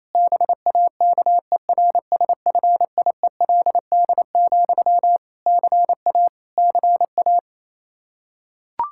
Morse code, heard as a keyed tone, about 28 wpm, 700 Hz. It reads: BAKERSFIELD, CA CA
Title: Practice QSO - Exchange 1 - Beginning - 28wpm